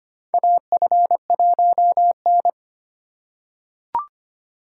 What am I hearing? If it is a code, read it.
Morse code, heard as a keyed tone, about 25 wpm, 700 Hz. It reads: AF1N
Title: USA Call Signs - 3x Character Spacing - 25wpm